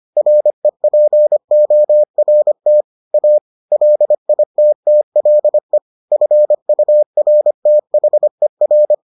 Morse code, heard as a keyed tone, about 25 wpm, 600 Hz. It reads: REPORT A LITTLE FURTHER